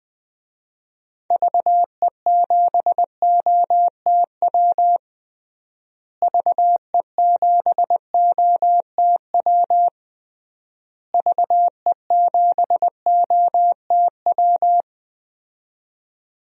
Morse code, heard as a keyed tone, about 21 wpm, 700 Hz. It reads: VE7OTW VE7OTW VE7OTW
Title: DXCC-100 Call Signs - 6 Characters - Repeated 3x - 20wpm